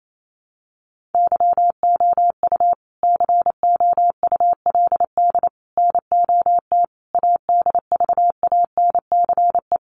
Morse code, heard as a keyed tone, about 28 wpm, 700 Hz. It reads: YOU COULD NOT ADVANCE
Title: Sentences from Top 1400 Words 28wpm